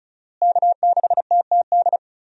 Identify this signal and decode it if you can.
Morse code, heard as a keyed tone, about 36 wpm, 700 Hz. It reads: K6TTD
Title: USA Call Signs - 5 Characters - Repeated 2x - 35wpm